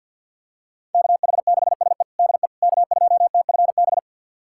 Morse code, heard as a keyed tone, about 52 wpm, 700 Hz. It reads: KH6SE DE K2TFB
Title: USA Callsign DE Callsign - 5 Characters - 50wpm